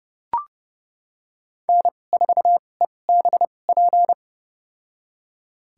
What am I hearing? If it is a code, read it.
Morse code, heard as a keyed tone, about 30 wpm, spaced as for 22 wpm, 700 Hz. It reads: N4EBP
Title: USA Call Signs - 5 Characters - 2x Character Spacing - 30wpm